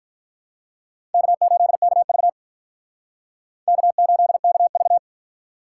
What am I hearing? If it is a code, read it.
Morse code, heard as a keyed tone, about 47 wpm, 700 Hz. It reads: K8KV K8KV